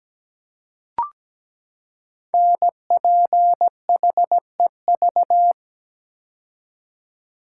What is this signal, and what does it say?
Morse code, heard as a keyed tone, about 17 wpm, 700 Hz. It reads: NPHEV
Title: Code Groups - 5 Letters - 17wpm